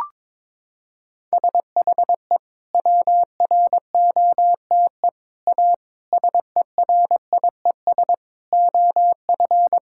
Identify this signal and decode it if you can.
Morse code, heard as a keyed tone, about 22 wpm, 700 Hz. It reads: SHE WROTE A SERIES OF
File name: practice / 22wpm / Sentences with 3-Word Phrases - 22wpm